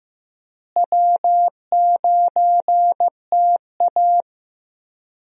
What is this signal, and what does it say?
Morse code, heard as a keyed tone, about 15 wpm, 700 Hz. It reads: W9TA
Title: USA Call Signs - 4 Characters - Repeated 3x - 15wpm